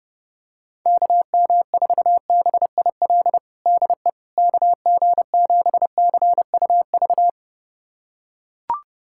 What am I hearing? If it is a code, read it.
Morse code, heard as a keyed tone, about 30 wpm, 700 Hz. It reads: KM4BIL DE KG7CUV
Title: USA Callsign DE Callsign - 6 Characters - 30wpm